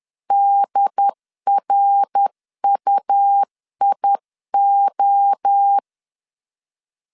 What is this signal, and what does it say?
Morse code, heard as a keyed tone, about 10 wpm, 800 Hz. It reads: DRUIO
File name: CW / DET / 1 6 1500C S C 10wpm 4252